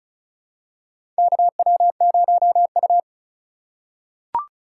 Morse code, heard as a keyed tone, about 36 wpm, 700 Hz. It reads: KW0U
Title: USA Call Signs - 4 Characters - 4x Character Spacing - 35wpm